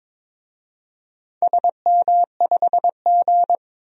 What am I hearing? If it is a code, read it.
Morse code, heard as a keyed tone, about 22 wpm, 700 Hz. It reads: SM5G
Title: DXCC-100 Call Signs - 4 Characters - 22wpm